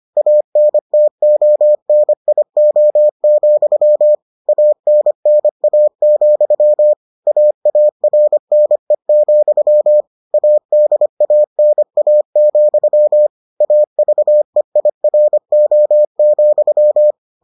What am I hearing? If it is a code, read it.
Morse code, heard as a keyed tone, about 25 wpm, 600 Hz. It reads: ANTONIO, ANNA, AARNE, ADANA, AVEIRO,